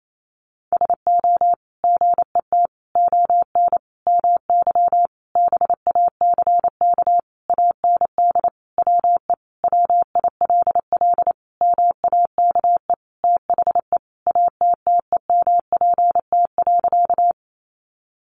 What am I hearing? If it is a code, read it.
Morse code, heard as a keyed tone, about 28 wpm, 700 Hz. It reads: SO GET ON MY BACK AND WE WILL MAKE THE ATTEMPT.
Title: Sentences from Top 1000 Words 28wpm